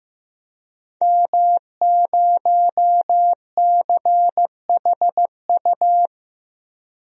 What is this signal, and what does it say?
Morse code, heard as a keyed tone, about 15 wpm, 700 Hz. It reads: M0CHU
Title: DXCC-100 Call Signs - 5 Characters - Repeated 2x - 15wpm